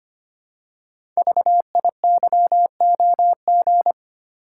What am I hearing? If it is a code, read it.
Morse code, heard as a keyed tone, about 25 wpm, 700 Hz. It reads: VIYOG
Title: Code Groups - 5 Letters - 25wpm